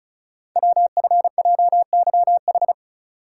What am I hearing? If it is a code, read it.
Morse code, heard as a keyed tone, about 36 wpm, 700 Hz. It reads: WFJYH
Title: Code Groups - 5 Letters - 35wpm